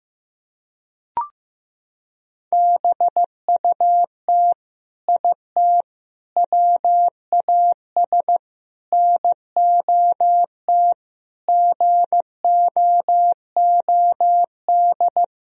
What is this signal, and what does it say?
Morse code, heard as a keyed tone, about 15 wpm, 700 Hz. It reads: BUT IT WAS NOT GOOD
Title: Sentences from Top 1000 Words - Review Part 4 - 15wpm